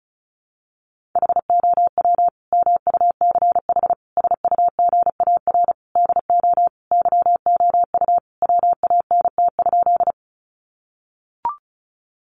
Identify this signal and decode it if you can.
Morse code, heard as a keyed tone, about 34 wpm, 700 Hz. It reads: HOW MUCH SUGAR DO YOU WANT?
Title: Sentences from Top 1200 Words 35wpm